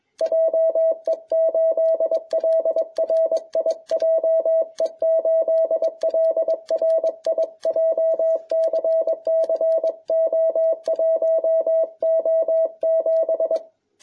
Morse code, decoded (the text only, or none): JE8LRI JE8LRI JCC O1O7